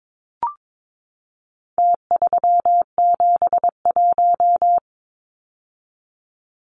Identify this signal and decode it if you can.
Morse code, heard as a keyed tone, about 22 wpm, 700 Hz. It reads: T371
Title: Numbers - Contest Style - 4-Digit 22wpm